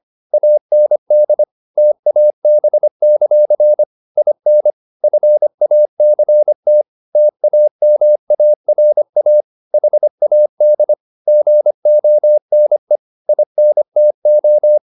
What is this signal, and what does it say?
Morse code, heard as a keyed tone, about 25 wpm, 600 Hz. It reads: AND TAB; IN FACT TAMARA HAD GONE INTO